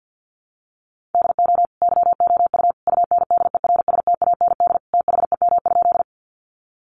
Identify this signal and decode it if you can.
Morse code, heard as a keyed tone, about 50 wpm, 700 Hz. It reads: DO YOU UNDERSTAND THEM?
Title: Sentences from Top 600 Words 50wpm